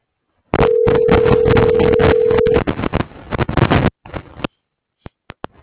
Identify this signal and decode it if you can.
Ringback tone (North America), 440 + 480 Hz.